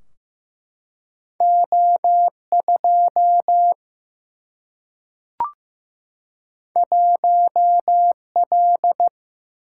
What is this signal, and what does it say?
Morse code, heard as a keyed tone, about 15 wpm, 700 Hz. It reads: O2 1L
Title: Single-Letter Single-Number 15wpm